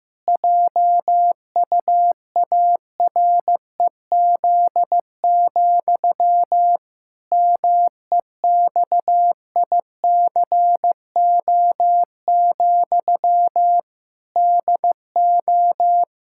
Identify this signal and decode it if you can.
Morse code, heard as a keyed tone, about 15 wpm, 700 Hz. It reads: JUAREZ, MEXICO, DO
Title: Pangram Challenge v2 - 15wpm